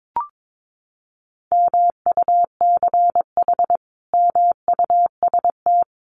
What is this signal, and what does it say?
Morse code, heard as a keyed tone, about 22 wpm, 700 Hz. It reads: MUCH MUST
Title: Sentences from Top 1400 Words 22wpm